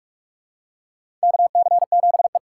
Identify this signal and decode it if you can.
Morse code, heard as a keyed tone, about 42 wpm, 700 Hz. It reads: KC7E
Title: USA Call Signs - 4 Characters - 2x Character Spacing - 45wpm